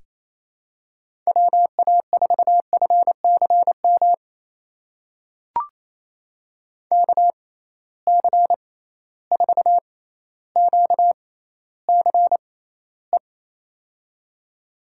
Morse code, heard as a keyed tone, about 27 wpm, spaced as for 14 wpm, 700 Hz. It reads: WA4FCM KC4QCE
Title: USA Call Signs - 6 Characters - 6x Character Spacing - 28wpm